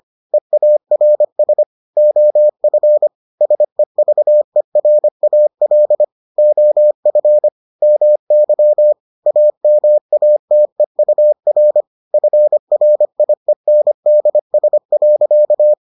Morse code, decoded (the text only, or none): EARS OF SEVERAL OF MY AMATEUR FRIENDS.